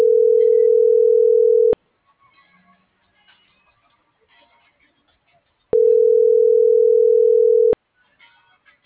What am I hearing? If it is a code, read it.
Ringback tone (North America), 440 + 480 Hz.